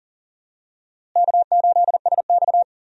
Morse code, heard as a keyed tone, about 40 wpm, 700 Hz. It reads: K8SX
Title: USA Call Signs - 4 Characters - Repeated 3x - 40wpm